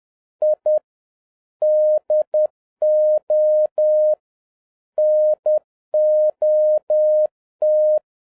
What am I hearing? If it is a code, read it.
Morse code, heard as a keyed tone, about 10 wpm, 610 Hz. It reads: I DO NOT